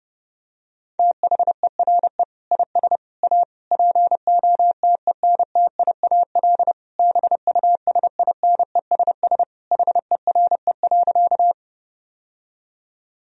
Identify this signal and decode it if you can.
Morse code, heard as a keyed tone, about 30 wpm, 700 Hz. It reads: THERE IS A POTENTIAL BUSINESS HERE.
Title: Sentences from Top 1000 Words - Review Part 4 - 30wpm